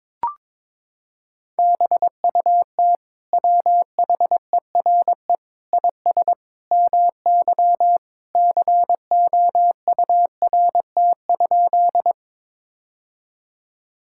Morse code, heard as a keyed tone, about 22 wpm, 700 Hz. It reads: BUT WHERE IS MY COURT?
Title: Sentences from Top 1600 Words 22wpm